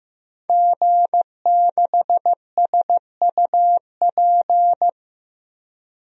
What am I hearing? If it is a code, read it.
Morse code, heard as a keyed tone, about 15 wpm, 700 Hz. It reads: G6SUP
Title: DXCC-100 Call Signs - 5 Characters - 15wpm